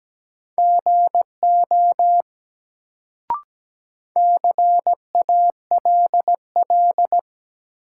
Morse code, heard as a keyed tone, about 17 wpm, 700 Hz. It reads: GO CALL
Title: Sentences from Top 100 Words - New Word Drill - 17wpm